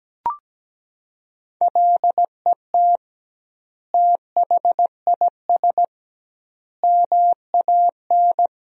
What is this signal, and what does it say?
Morse code, heard as a keyed tone, about 17 wpm, 700 Hz. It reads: LET THIS MAN
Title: 2x Word-Spacing - Sentences from Top 400 Words - 17wpm